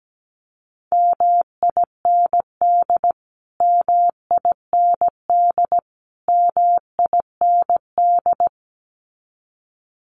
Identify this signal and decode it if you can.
Morse code, heard as a keyed tone, about 17 wpm, 700 Hz. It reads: MIND MIND MIND
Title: Top 500 Words Repeated 17wpm